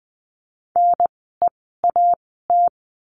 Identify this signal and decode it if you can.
Morse code, heard as a keyed tone, about 20 wpm, spaced as for 15 wpm, 700 Hz. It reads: NEAT
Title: 030 Lesson 6f - Letter i - Words 20wpm